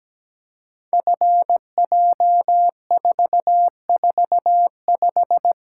Morse code, heard as a keyed tone, about 17 wpm, 700 Hz. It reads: FJ445
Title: Code Groups - 5 Letter:Numbers - 17wpm